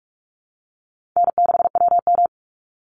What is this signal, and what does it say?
Morse code, heard as a keyed tone, about 44 wpm, 700 Hz. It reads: N6WM
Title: Easy Call Signs Repeated 2x - 45wpm